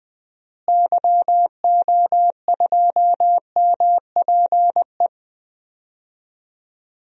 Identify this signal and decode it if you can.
Morse code, heard as a keyed tone, about 20 wpm, 700 Hz. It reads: YO2MPE
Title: DXCC-100 Call Signs - 6 Characters - Repeated 2x - 20wpm